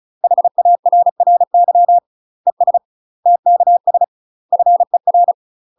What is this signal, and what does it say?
Morse code, heard as a keyed tone, about 36 wpm, 700 Hz. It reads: HARRY ES TKS FER
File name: UK MASTER QSOs TRAINING / 35WPM / UK MASTER QSOs 0005